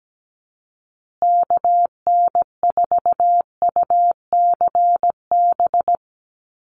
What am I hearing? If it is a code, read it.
Morse code, heard as a keyed tone, about 17 wpm, 700 Hz. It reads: KN4UCB
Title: USA Call Signs Repeated 2x 17wpm